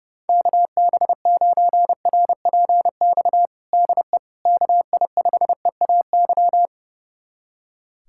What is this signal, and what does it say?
Morse code, heard as a keyed tone, about 30 wpm, 700 Hz. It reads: KB9RPX DE KI5EAY
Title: USA Callsign DE Callsign - 6 Characters - 30wpm